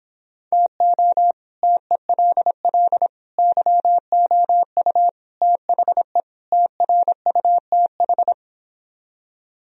Morse code, heard as a keyed tone, about 26 wpm, 700 Hz. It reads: TO TELL YOU THE TRUTH